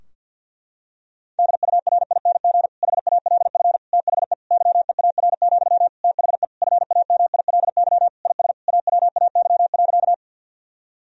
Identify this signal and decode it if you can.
Morse code, heard as a keyed tone, about 46 wpm, 700 Hz. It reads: DURING HALF THE YEAR, THE FAMILY IS AWAY.
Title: Sentences from Top 1000 Words - Review Part 2 - 50wpm